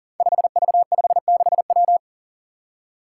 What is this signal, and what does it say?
Morse code, heard as a keyed tone, about 39 wpm, 700 Hz. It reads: 5V56W
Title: Five Challenging Letter-Number Combinations 40wpm